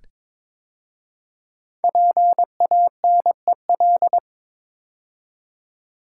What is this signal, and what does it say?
Morse code, heard as a keyed tone, about 22 wpm, 700 Hz. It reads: PANEL
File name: 5 Letter Words - First 3 Letters Given - 22wpm